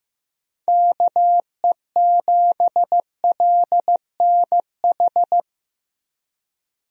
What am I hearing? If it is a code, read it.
Morse code, heard as a keyed tone, about 15 wpm, 700 Hz. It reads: KE7LNH
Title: USA Call Signs - 6 Characters - Repeated 3x - 15wpm